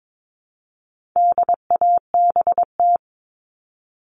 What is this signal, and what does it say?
Morse code, heard as a keyed tone, about 22 wpm, 700 Hz. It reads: DABT
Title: All Cut Numbers - 4-Digit 22wpm